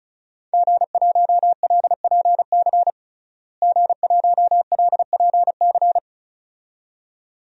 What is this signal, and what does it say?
Morse code, heard as a keyed tone, about 36 wpm, 700 Hz. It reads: G1LPC G1LPC
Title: DXCC-100 Call Signs - 5 Characters - Repeated 2x - 35wpm